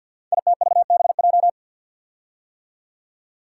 Morse code, heard as a keyed tone, about 52 wpm, 700 Hz. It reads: ITVBJ
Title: Code Groups - 5 Letters - 50wpm